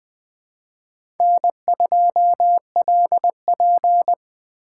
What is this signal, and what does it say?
Morse code, heard as a keyed tone, about 23 wpm, 700 Hz. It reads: N2LP